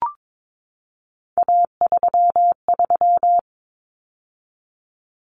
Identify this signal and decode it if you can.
Morse code, heard as a keyed tone, about 22 wpm, 700 Hz. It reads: A33
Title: Common Cut Numbers - 3-Digit 22wpm